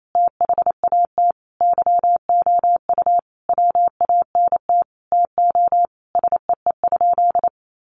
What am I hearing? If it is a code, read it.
Morse code, heard as a keyed tone, about 28 wpm, 700 Hz. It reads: THAT YOU WANT TO SEE?
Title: Sentences from Top 1100 Words 28wpm